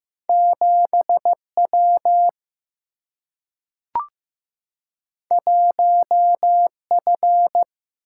Morse code, heard as a keyed tone, about 15 wpm, 700 Hz. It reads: 7 W 1 F
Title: Single-Letter Single-Number 15wpm